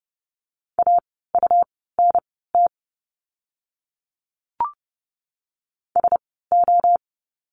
Morse code, heard as a keyed tone, about 30 wpm, spaced as for 17 wpm, 700 Hz. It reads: AUNT SO
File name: Lesson 17f - Letter u - Words 30wpm